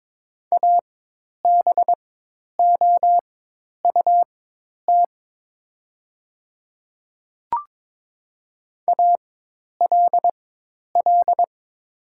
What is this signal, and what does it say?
Morse code, heard as a keyed tone, about 22 wpm, spaced as for 10 wpm, 700 Hz. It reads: ABOUT ALL
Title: 4x Character Spacing - Top 100 Words - 22wpm